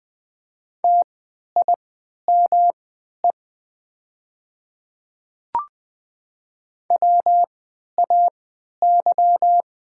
Morse code, heard as a keyed tone, about 20 wpm, spaced as for 11 wpm, 700 Hz. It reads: TIME WAY
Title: 3x Character Spacing - Top 100 Words - 20wpm